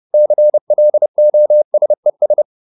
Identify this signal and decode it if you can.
Morse code, heard as a keyed tone, about 30 wpm, 600 Hz. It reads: CLOSES